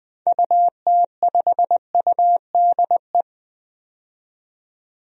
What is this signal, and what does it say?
Morse code, heard as a keyed tone, about 20 wpm, 700 Hz. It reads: UT5UDE
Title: DXCC-100 Call Signs - 6 Characters - 20wpm